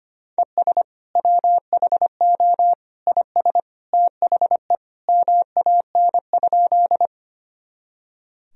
Morse code, heard as a keyed tone, about 25 wpm, 700 Hz. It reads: ES WHO IS THE MAN?
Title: Sentences from Top 200 Words and Abbreviations 25wpm